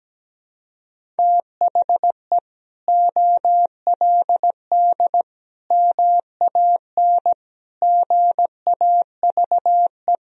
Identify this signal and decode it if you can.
Morse code, heard as a keyed tone, about 17 wpm, 700 Hz. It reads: THE OLD MAN GAVE
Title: Sentences from Top 500 Words and Abbreviations 17wpm